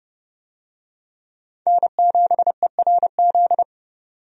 Morse code, heard as a keyed tone, about 29 wpm, 700 Hz. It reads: N7ERZ
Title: USA Call Signs - 5 Characters - 30wpm